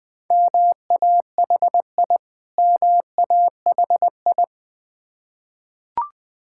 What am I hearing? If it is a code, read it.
Morse code, heard as a keyed tone, about 20 wpm, 700 Hz. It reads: MAHI MAHI
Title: Difficult Words with Hints - 20wpm